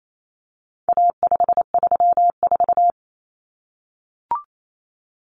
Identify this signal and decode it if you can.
Morse code, heard as a keyed tone, about 28 wpm, 700 Hz. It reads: A534
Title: Common Cut Numbers - 4-Digit 28wpm